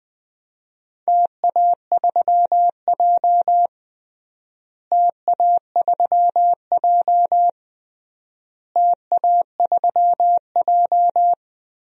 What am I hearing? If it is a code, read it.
Morse code, heard as a keyed tone, about 21 wpm, 700 Hz. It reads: TA3J TA3J TA3J
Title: DXCC-100 Call Signs - 4 Characters - Repeated 3x - 20wpm